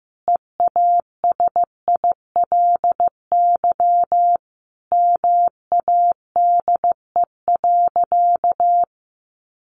Morse code, heard as a keyed tone, about 15 wpm, 700 Hz. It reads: EASILY MADE.
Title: Sentences from Top 1200 Words 15wpm